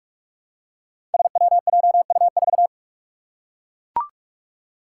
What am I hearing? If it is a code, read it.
Morse code, heard as a keyed tone, about 44 wpm, 700 Hz. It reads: SWJU4
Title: Five Challenging Letter-Number Combinations 45wpm